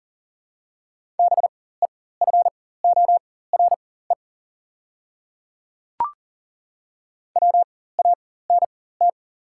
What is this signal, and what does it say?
Morse code, heard as a keyed tone, about 40 wpm, spaced as for 19 wpm, 700 Hz. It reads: BEFORE WANT
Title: 4x Character Spacing - Top 300 Words - 40wpm